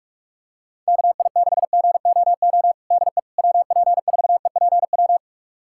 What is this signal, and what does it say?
Morse code, heard as a keyed tone, about 41 wpm, 700 Hz. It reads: KI6GOO DE WP4EPW
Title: USA Callsign DE Callsign - 6 Characters - 45wpm